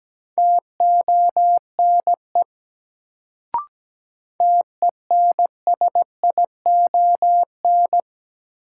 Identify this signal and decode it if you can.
Morse code, heard as a keyed tone, about 17 wpm, 700 Hz. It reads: TONE TENSION
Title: Sentences from Top 1600 Words - New Word Drill Alt - 17wpm